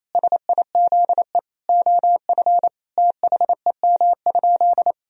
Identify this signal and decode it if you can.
Morse code, heard as a keyed tone, about 28 wpm, 700 Hz. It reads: SIZE OF THEM?
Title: Sentences from Top 1000 Words - Review Part 2 - 28wpm